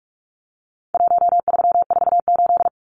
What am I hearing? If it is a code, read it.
Morse code, heard as a keyed tone, about 44 wpm, 700 Hz. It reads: 1348
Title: Numbers - Contest Style - 4-Digit 45wpm